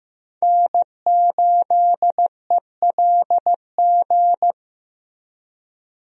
Morse code, heard as a keyed tone, about 15 wpm, 700 Hz. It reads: N8ELG
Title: USA Call Signs - 5 Characters - Repeated 3x - 15wpm